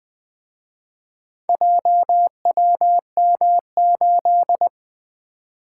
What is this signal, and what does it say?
Morse code, heard as a keyed tone, about 24 wpm, 700 Hz. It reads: JWM8
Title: Speed-Racing - Four Letter-Number Combinations - 20wpm